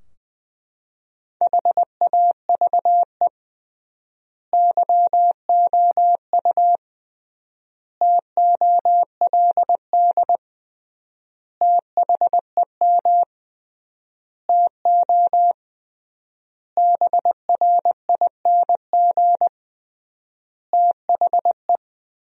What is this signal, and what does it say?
Morse code, heard as a keyed tone, about 20 wpm, 700 Hz. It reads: HAVE YOU TOLD THEM TO BRING THE
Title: 3x Word-Spacing - Sentences from Top 500 Words - 20wpm